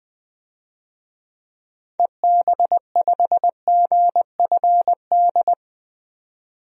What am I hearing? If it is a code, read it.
Morse code, heard as a keyed tone, about 20 wpm, 700 Hz. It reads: EB5GFD
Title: DXCC-100 Call Signs - 6 Characters - 20wpm